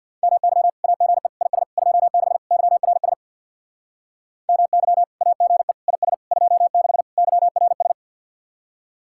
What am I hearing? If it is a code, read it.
Morse code, heard as a keyed tone, about 64 wpm, 700 Hz. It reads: MY AGE IS 26 YRS MY AGE IS 26 YRS